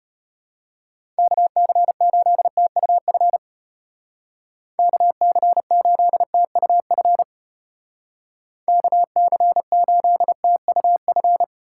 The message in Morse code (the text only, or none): KC8TUF KC8TUF KC8TUF